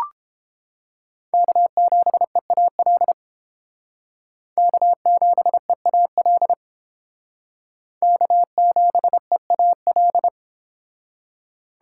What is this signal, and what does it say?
Morse code, heard as a keyed tone, about 31 wpm, 700 Hz. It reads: K7EAL K7EAL K7EAL